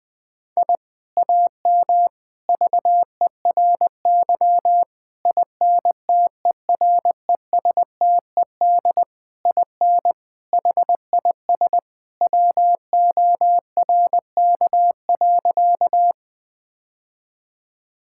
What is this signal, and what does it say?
Morse code, heard as a keyed tone, about 20 wpm, 700 Hz. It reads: I AM VERY INTERESTED IN HIS WORK.
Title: Sentences from Top 1000 Words - Review Part 6 - 20wpm